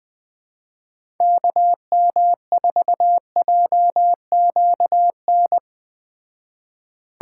Morse code, heard as a keyed tone, about 21 wpm, 700 Hz. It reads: KM4JQN
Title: Speed Racing - USA 2x3 Call Signs 17wpm